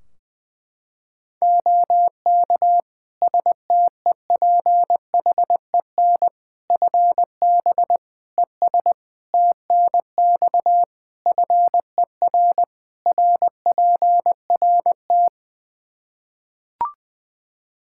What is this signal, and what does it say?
Morse code, heard as a keyed tone, about 20 wpm, 700 Hz. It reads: OK STEPHEN FB ES TNX FER RPRT
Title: Practice QSO - Exchange 2 - Info - 20wpm